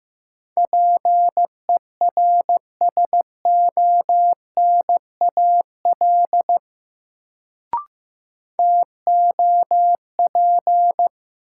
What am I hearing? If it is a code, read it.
Morse code, heard as a keyed tone, about 15 wpm, 700 Hz. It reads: PERSONAL TOP
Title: Sentences from Top 700 Words - New Word Drill - 15wpm